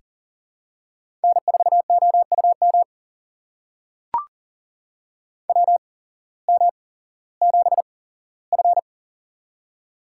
Morse code, heard as a keyed tone, about 40 wpm, spaced as for 19 wpm, 700 Hz. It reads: N4OUM WM7F
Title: USA Call Signs - 8x Character Spacing - 40wpm